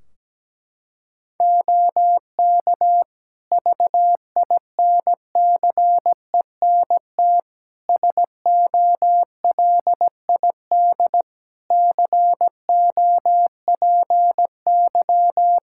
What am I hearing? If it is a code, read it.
Morse code, heard as a keyed tone, about 17 wpm, 700 Hz. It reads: OK VINCENT SOLID COPY
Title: Practice QSO - Exchange 3 - More Info - 17wpm